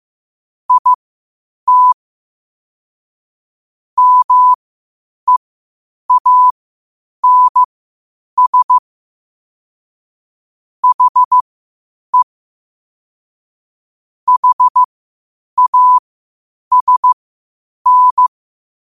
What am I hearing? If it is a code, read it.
Morse code, heard as a keyed tone, about 15 wpm, spaced as for 8 wpm, 1000 Hz. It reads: IT MEANS HE HASN
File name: qotdfiles / 8wpm